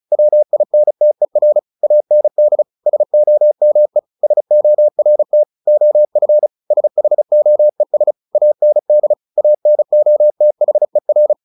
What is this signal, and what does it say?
Morse code, heard as a keyed tone, about 34 wpm, 600 Hz. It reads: WINTER AND SOME SORT OF SHOES AND ANOTHER
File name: the old man and the sea / the old man and the sea 1 35wpm